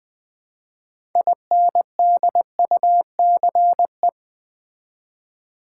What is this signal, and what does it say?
Morse code, heard as a keyed tone, about 20 wpm, 700 Hz. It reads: INDUCE
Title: Build A Word - 3 Letters At A Time - 6-Letter Words - 20wpm